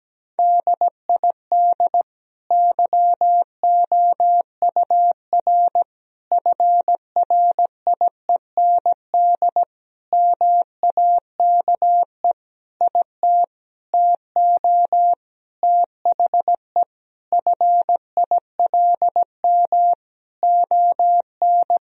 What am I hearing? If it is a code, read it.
Morse code, heard as a keyed tone, about 17 wpm, 700 Hz. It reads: DID YOUR FRIEND MAKE IT TO THE FILM ON
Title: Sentences from Top 1000 Words - Review Part 3 - 17wpm